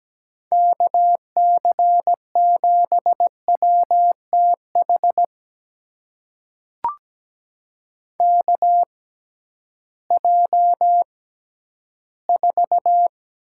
Morse code, heard as a keyed tone, about 17 wpm, spaced as for 11 wpm, 700 Hz. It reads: KC7WTH KJ4
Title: USA Call Signs - 6 Characters - 6x Character Spacing - 17wpm